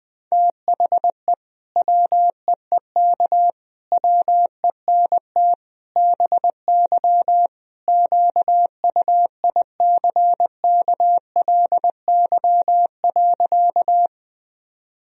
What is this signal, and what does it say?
Morse code, heard as a keyed tone, about 20 wpm, 700 Hz. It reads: THE WEEK WENT BY QUICKLY.
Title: Sentences from Top 1000 Words - Review Part 2 - 20wpm